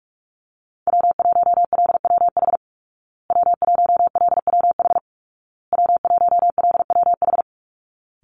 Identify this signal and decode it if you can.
Morse code, heard as a keyed tone, about 44 wpm, 700 Hz. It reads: W1LWH W1LWH W1LWH
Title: Call Signs Repeated 2x - 45wpm